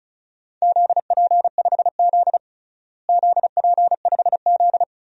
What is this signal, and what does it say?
Morse code, heard as a keyed tone, about 36 wpm, 700 Hz. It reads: ZP5Z ZP5Z
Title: DXCC-100 Call Signs - 4 Characters - Repeated 3x - 35wpm